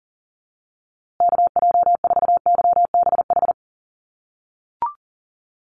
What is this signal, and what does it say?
Morse code, heard as a keyed tone, about 40 wpm, 700 Hz. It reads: KJ4YBH
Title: USA Call Signs Repeated 2x 40wpm